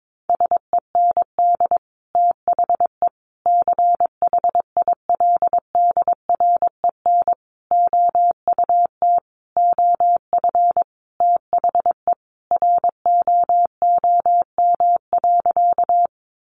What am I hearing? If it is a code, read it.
Morse code, heard as a keyed tone, about 22 wpm, 700 Hz. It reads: SEND THE CHILDREN OUT OF THE ROOM.
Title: Sentences from Top 1500 Words 22wpm